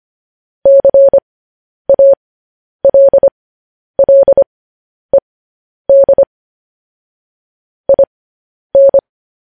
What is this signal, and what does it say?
Morse code, heard as a keyed tone, about 25 wpm, spaced as for 10 wpm, 550 Hz. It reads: CALLED IN
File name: podcast / News headlines-1x-25x10